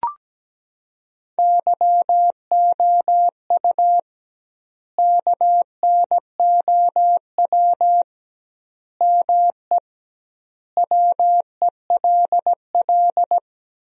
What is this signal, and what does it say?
Morse code, heard as a keyed tone, about 17 wpm, 700 Hz. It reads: YOU KNOW ME WELL
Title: 2x Word-Spacing - Sentences from Top 400 Words - 17wpm